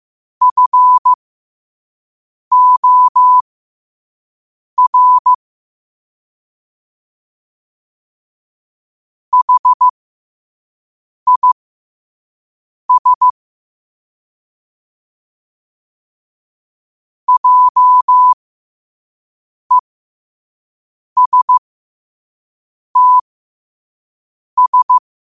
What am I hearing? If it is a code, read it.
Morse code, heard as a keyed tone, about 15 wpm, spaced as for 5 wpm, 1000 Hz. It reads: FOR HIS JESTS
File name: qotdfiles / 5wpm